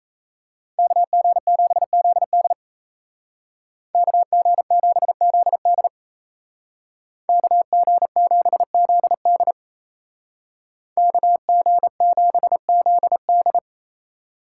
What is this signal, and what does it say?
Morse code, heard as a keyed tone, about 37 wpm, 700 Hz. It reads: KG7ZD KG7ZD KG7ZD KG7ZD